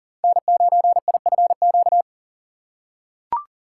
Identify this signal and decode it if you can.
Morse code, heard as a keyed tone, about 37 wpm, 700 Hz. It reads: N9IFQ
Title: USA Call Signs - 5 Characters - 6x Character Spacing - 40wpm